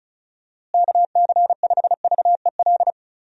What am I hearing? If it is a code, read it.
Morse code, heard as a keyed tone, about 36 wpm, 700 Hz. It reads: KC5VEL
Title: USA Call Signs - 6 Characters - Repeated 3x - 35wpm